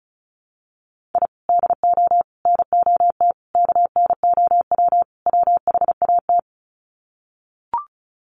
Morse code, heard as a keyed tone, about 34 wpm, 700 Hz. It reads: I DO NOT KNOW WHAT
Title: Sets of 5 Words 35wpm